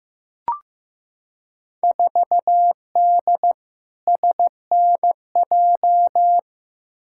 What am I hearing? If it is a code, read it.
Morse code, heard as a keyed tone, about 15 wpm, 700 Hz. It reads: 4D SNJ
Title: Field Day Exchanges Encore - 15wpm